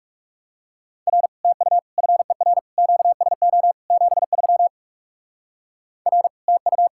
Morse code, heard as a keyed tone, about 36 wpm, 700 Hz. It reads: R TU FER QSO 73 R TU
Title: Speed-Racing Common QSO Phrases - 30wpm